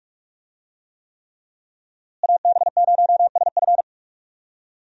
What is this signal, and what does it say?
Morse code, heard as a keyed tone, about 44 wpm, 700 Hz. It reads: AB0SF
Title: USA Call Signs - 5 Characters - Repeated 3x - 45wpm